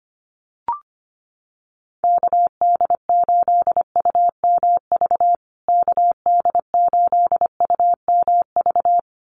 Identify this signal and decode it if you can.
Morse code, heard as a keyed tone, about 25 wpm, 700 Hz. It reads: KD8UMV KD8UMV
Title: USA Call Signs Repeated 2x 25wpm